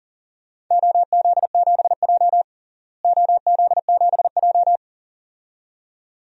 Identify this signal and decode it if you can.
Morse code, heard as a keyed tone, about 40 wpm, 700 Hz. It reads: OZ7J OZ7J
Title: DXCC-100 Call Signs - 4 Characters - Repeated 2x - 40wpm